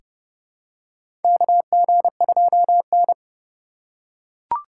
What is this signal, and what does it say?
Morse code, heard as a keyed tone, about 30 wpm, 700 Hz. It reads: KG2N
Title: USA Call Signs - 3x Character Spacing - 30wpm